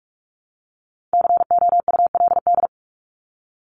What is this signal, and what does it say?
Morse code, heard as a keyed tone, about 44 wpm, 700 Hz. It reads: COULD
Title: Top 500 Words Repeated 45wpm